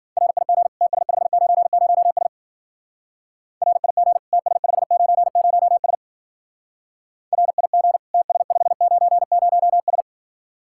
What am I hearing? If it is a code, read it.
Morse code, heard as a keyed tone, about 49 wpm, 700 Hz. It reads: RIG TS590S RIG TS590S RIG TS590S
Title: Speed-Racing Common QSO Phrases Encore - 40wpm